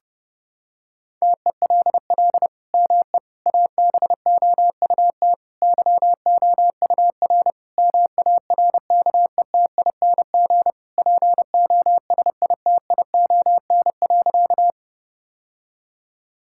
Morse code, heard as a keyed tone, about 30 wpm, 700 Hz. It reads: TELL ME ABOUT YOUR MARKETING POSITION.
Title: Sentences from Top 1000 Words - Review Part 5 - 30wpm